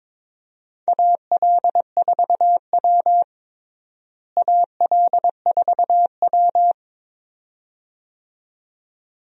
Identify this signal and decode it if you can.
Morse code, heard as a keyed tone, about 23 wpm, 700 Hz. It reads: AL4W AL4W
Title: USA Call Signs - 4 Characters - Repeated 2x - 22wpm